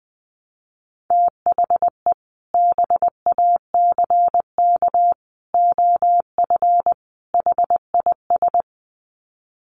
Morse code, heard as a keyed tone, about 20 wpm, 700 Hz. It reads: THE BACK OF HIS
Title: Sets of 4 Words 20wpm